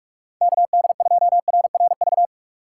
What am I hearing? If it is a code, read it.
Morse code, heard as a keyed tone, about 46 wpm, 700 Hz. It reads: KD2RRV
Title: USA Call Signs - 6 Characters - 4x Character Spacing - 45wpm